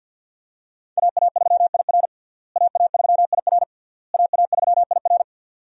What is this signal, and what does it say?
Morse code, heard as a keyed tone, about 53 wpm, 700 Hz. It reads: AA3IR AA3IR AA3IR
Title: USA Call Signs - 5 Characters - Repeated 3x - 50wpm